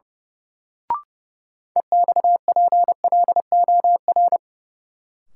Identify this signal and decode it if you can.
Morse code, heard as a keyed tone, about 30 wpm, 700 Hz. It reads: EXPLOR
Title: Sentences from Top 1100 Words - New Word Drill Alt - 30wpm